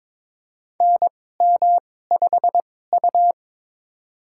Morse code, heard as a keyed tone, about 22 wpm, spaced as for 16 wpm, 700 Hz. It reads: NM5U
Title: USA Call Signs - 4 Characters - 2x Character Spacing - 22wpm